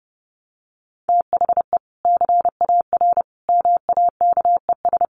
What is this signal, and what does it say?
Morse code, heard as a keyed tone, about 30 wpm, 700 Hz. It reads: THE CAR MAKES
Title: Easy Phrases 30wpm